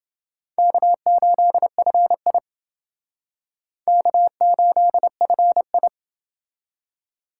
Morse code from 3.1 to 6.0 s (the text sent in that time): K8FI